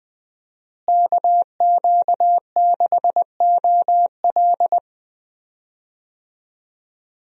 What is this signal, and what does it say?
Morse code, heard as a keyed tone, about 21 wpm, 700 Hz. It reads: KQ6OL
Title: USA Call Signs - 5 Characters - Repeated 3x - 20wpm